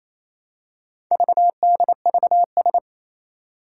Morse code, heard as a keyed tone, about 27 wpm, 700 Hz. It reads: VDVS
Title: Four Challenging Letter-Number Combinations 28wpm